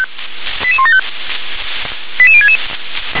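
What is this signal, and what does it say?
Five-tone selective call (ZVEI), heard as quick runs of five tones, tones of about 80 ms. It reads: 88175 88066
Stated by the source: ZVEI five-tone call 88018 received by SDR.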